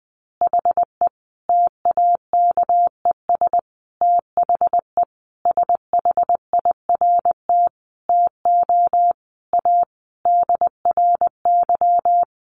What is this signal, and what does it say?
Morse code, heard as a keyed tone, about 20 wpm, 700 Hz. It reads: HE TAKES THE SHIRT TO A DRY
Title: Additional Easy Phrases 20wpm